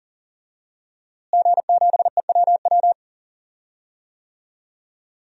Morse code, heard as a keyed tone, about 40 wpm, 700 Hz. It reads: G7EWW
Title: DXCC-100 Call Signs - 5 Characters - Repeated 2x - 40wpm